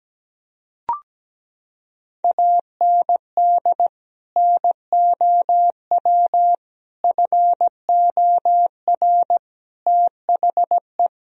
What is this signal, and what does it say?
Morse code, heard as a keyed tone, about 17 wpm, 700 Hz. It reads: AND NOW FOR THE
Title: Sentences from Top 1000 Words - Review Part 1 - 17wpm